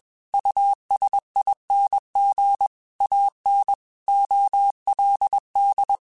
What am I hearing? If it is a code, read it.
Morse code, heard as a keyed tone, about 21 wpm, 800 Hz. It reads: USING AN OLD
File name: QSOs / QSOs-mp3s / WD0FBT 1@20